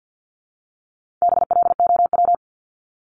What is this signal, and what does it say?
Morse code, heard as a keyed tone, about 50 wpm, 700 Hz. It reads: BLOW
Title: Top 1751-2000 Words 50wpm